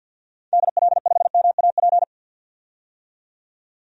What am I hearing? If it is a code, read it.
Morse code, heard as a keyed tone, about 50 wpm, 700 Hz. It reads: DL5MAP
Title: DXCC-100 Call Signs - 6 Characters - 50wpm